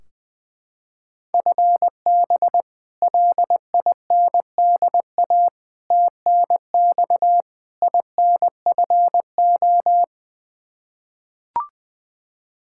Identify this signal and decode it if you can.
Morse code, heard as a keyed tone, about 20 wpm, 700 Hz. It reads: FB LINDA TNX INFO
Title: Practice QSO - Exchange 3 - More Info - 20wpm